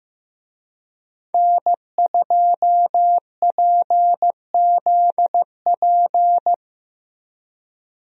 Morse code, heard as a keyed tone, about 15 wpm, 700 Hz. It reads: N2PZP
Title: USA Call Signs - 5 Characters - Repeated 2x - 15wpm